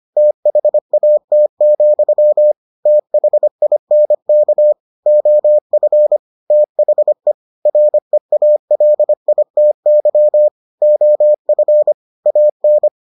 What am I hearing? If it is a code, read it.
Morse code, heard as a keyed tone, about 25 wpm, 600 Hz. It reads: THAT, THINK OF THE REALITY OF AN